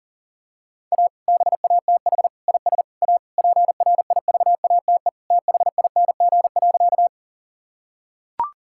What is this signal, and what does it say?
Morse code, heard as a keyed tone, about 37 wpm, 700 Hz. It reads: A BATH IS A PRIVATE THING.
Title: Sentences from Top 1600 Words 40wpm